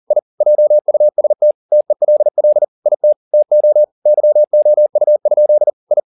Morse code, heard as a keyed tone, about 40 wpm, 600 Hz. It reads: I JUST TELL IT TO YOU? I